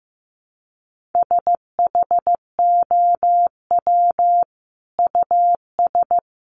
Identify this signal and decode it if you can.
Morse code, heard as a keyed tone, about 15 wpm, 700 Hz. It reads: SHOW US
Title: Sentences from Top 200 Words 15wpm